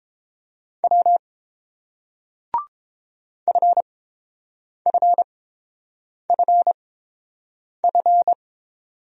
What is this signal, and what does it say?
Morse code, heard as a keyed tone, about 31 wpm, spaced as for 7 wpm, 700 Hz. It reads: W FFFF